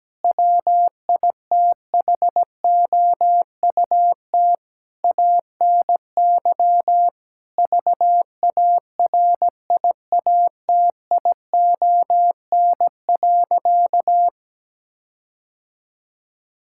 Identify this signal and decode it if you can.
Morse code, heard as a keyed tone, about 17 wpm, 700 Hz. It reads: WITHOUT ANY VARIATION.
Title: Sentences from Top 1600 Words 17wpm